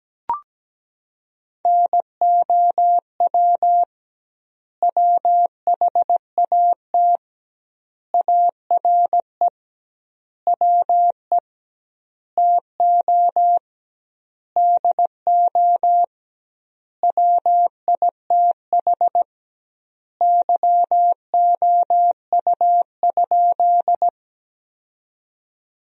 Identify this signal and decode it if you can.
Morse code, heard as a keyed tone, about 17 wpm, 700 Hz. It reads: NOW WHAT ARE WE TO DO WITH YOU?
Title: 2x Word-Spacing - Sentences from Top 100 Words - 17wpm